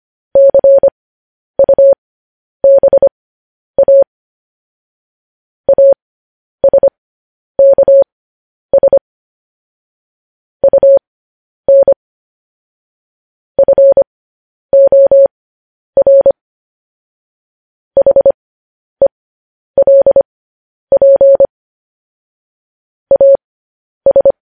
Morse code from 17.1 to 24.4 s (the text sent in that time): HELP AS